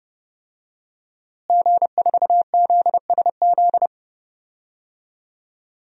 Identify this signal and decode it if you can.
Morse code, heard as a keyed tone, about 29 wpm, 700 Hz. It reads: G4ZSZ
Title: DXCC-100 Call Signs - 5 Characters - Repeated 3x - 30wpm